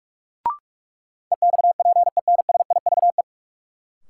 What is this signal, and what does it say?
Morse code, heard as a keyed tone, about 46 wpm, 700 Hz. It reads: EXPENSIVE
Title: Sentences from Top 1500 Words - New Word Drill Alt - 45wpm